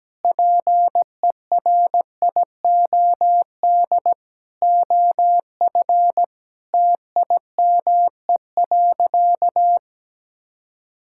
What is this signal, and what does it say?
Morse code, heard as a keyed tone, about 17 wpm, 700 Hz. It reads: PERIOD OF TIME.
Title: Sentences from Top 1000 Words - Review Part 5 - 17wpm